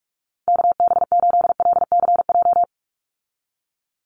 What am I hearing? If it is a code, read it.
Morse code, heard as a keyed tone, about 44 wpm, 700 Hz. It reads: KB8LCJ
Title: USA Call Signs 45wpm